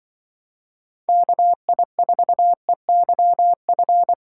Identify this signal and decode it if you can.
Morse code, heard as a keyed tone, about 24 wpm, 700 Hz. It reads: KI4EYF
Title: Speed Racing - USA 2x3 Call Signs 20wpm